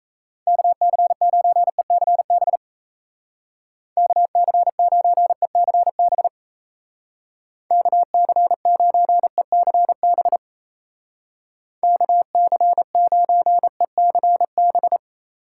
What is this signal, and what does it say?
Morse code, heard as a keyed tone, about 36 wpm, 700 Hz. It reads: KC9ECB KC9ECB KC9ECB KC9ECB